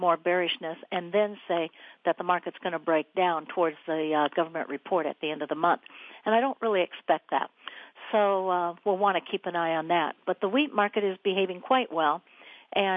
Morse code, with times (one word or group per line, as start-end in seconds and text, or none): none